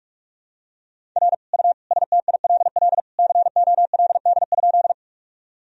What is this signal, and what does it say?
Morse code, heard as a keyed tone, about 46 wpm, 700 Hz. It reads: R U STILL COLD?
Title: Sentences from Top 400 Words and Abbreviations 45wpm